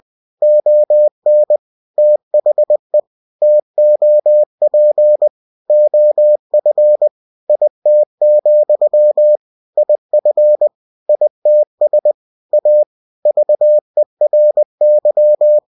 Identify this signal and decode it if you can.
Morse code, heard as a keyed tone, about 20 wpm, 600 Hz. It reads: ON THE TOP OF IT, IF ITS A VERY
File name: cw-books / winnie-the-pooh / winnie-20wpm-page-0095